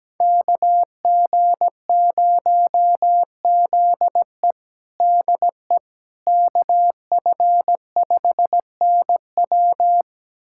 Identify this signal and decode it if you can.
Morse code, heard as a keyed tone, about 17 wpm, 700 Hz. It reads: KG0ZE DE KF5NW
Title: USA Callsign DE Callsign - 5 Characters - 17wpm